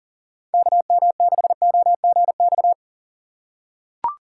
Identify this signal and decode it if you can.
Morse code, heard as a keyed tone, about 41 wpm, 700 Hz. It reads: KM6OGX
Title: USA Call Signs - 6 Characters - 8x Character Spacing - 40wpm